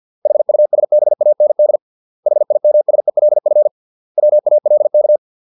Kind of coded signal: Morse code